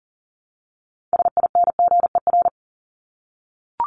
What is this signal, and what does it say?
Morse code, heard as a keyed tone, about 40 wpm, 700 Hz. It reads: SINGER
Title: Top 1501-1750 Words 40wpm